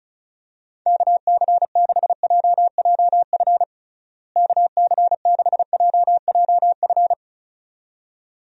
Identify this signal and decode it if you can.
Morse code, heard as a keyed tone, about 36 wpm, 700 Hz. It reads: KC6JJF KC6JJF
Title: USA Call Signs - 6 Characters - Repeated 2x - 35wpm